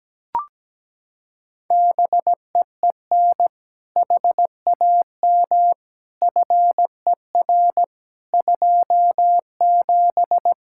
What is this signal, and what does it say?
Morse code, heard as a keyed tone, about 17 wpm, 700 Hz. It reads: BEEN HAM FER 27
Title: K1USN SST QSO Practice 17wpm